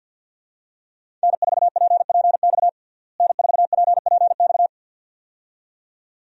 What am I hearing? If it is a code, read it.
Morse code, heard as a keyed tone, about 46 wpm, 700 Hz. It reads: N4PPX N4PPX
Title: USA Call Signs - 5 Characters - Repeated 2x - 50wpm